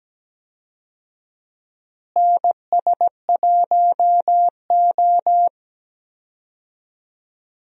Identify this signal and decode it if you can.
Morse code, heard as a keyed tone, about 17 wpm, 700 Hz. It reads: NS1O
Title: USA Call Signs - 4 Characters - Repeated 3x - 17wpm